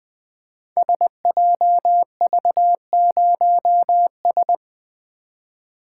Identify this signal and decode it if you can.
Morse code, heard as a keyed tone, about 20 wpm, 700 Hz. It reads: SJV0S
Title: Code Groups - 5 Letter:Numbers - 20wpm